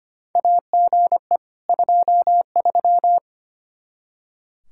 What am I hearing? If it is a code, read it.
Morse code, heard as a keyed tone, about 25 wpm, 700 Hz. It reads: AGE 23
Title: K1USN SST QSO Practice 25wpm